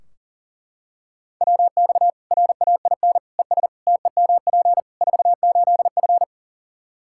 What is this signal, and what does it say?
Morse code, heard as a keyed tone, about 40 wpm, 700 Hz. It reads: WX RAIN ES TEMP 48F
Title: Practice QSO - Exchange 2 - Info - 40wpm